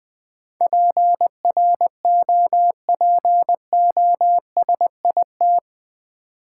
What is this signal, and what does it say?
Morse code, heard as a keyed tone, about 20 wpm, 700 Hz. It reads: PROPOSIT
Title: Build A Word - 4 Letters At A Time - 11-Letter Words - 20wpm